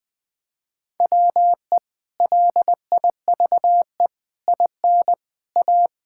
Morse code, heard as a keyed tone, about 19 wpm, 700 Hz. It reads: WE LIVE IN A
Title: Sentences from Top 1000 Words - Review Part 5 - 20wpm